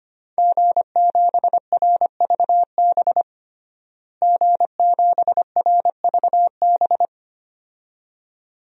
Morse code, heard as a keyed tone, about 25 wpm, 700 Hz. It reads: G7RVB G7RVB
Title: DXCC-100 Call Signs - 5 Characters - Repeated 2x - 25wpm